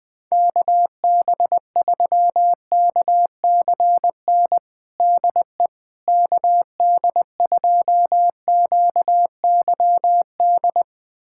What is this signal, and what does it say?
Morse code, heard as a keyed tone, about 20 wpm, 700 Hz. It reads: KB3KCN DE KD2QYD
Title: USA Callsign DE Callsign - 6 Characters - 20wpm